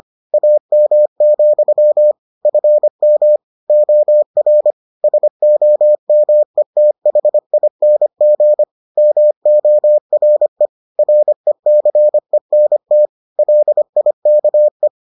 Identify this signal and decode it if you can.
Morse code, heard as a keyed tone, about 25 wpm, 600 Hz. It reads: AM, FM OR SOMETHING MORE RECENT LIKE